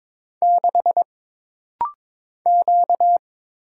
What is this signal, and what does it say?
Morse code, heard as a keyed tone, about 22 wpm, 700 Hz. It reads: 6 Q
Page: This rapid-fire timing is geared to push your ICR (Instant Character Recognition) skills.